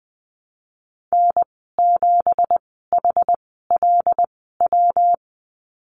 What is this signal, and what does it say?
Morse code, heard as a keyed tone, about 20 wpm, 700 Hz. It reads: N 7 H L W
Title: 207 Lesson 33g - Letter j - Call Signs 20wpm